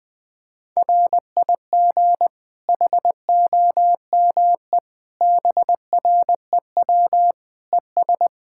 Morse code, heard as a keyed tone, about 20 wpm, 700 Hz. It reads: RIG HOME BREW ES
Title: Practice QSO - Exchange 2 - Info - 20wpm